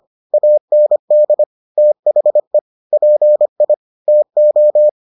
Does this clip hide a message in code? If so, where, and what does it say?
Yes: Morse code from 0.3 to 4.9 s, heard as a keyed tone, about 25 wpm, 600 Hz. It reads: AND THE PI TO